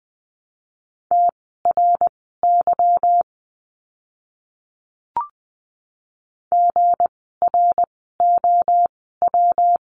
Morse code, heard as a keyed tone, about 20 wpm, spaced as for 15 wpm, 700 Hz. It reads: TRY GROW
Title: 160 Lesson 26f - Letter g - Words 20wpm